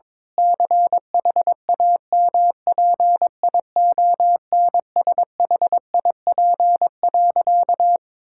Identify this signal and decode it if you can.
Morse code, heard as a keyed tone, about 22 wpm, 700 Hz. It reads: CHAMPIONSHIP.
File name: practice / 22wpm / Sentences with 8-Word Phrases - 22wpm